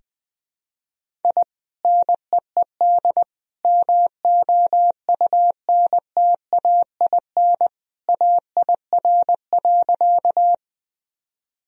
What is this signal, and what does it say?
Morse code, heard as a keyed tone, about 20 wpm, 700 Hz. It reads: I NEED MOUNTAIN AIR.
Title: Sentences from Top 1000 Words - Review Part 1 - 20wpm